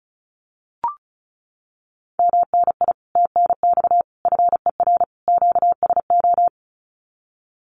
Morse code, heard as a keyed tone, about 36 wpm, 700 Hz. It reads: MNI TNX FER QSO
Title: Common QSO Phrases Encore 35wpm